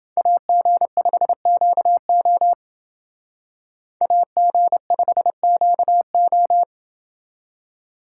Morse code, heard as a keyed tone, about 28 wpm, 700 Hz. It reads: AG5QO AG5QO